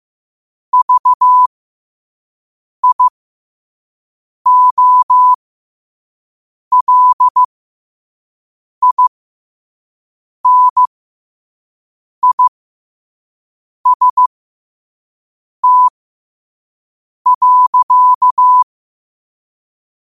Morse code, heard as a keyed tone, about 15 wpm, spaced as for 5 wpm, 1000 Hz. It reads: VIOLINIST.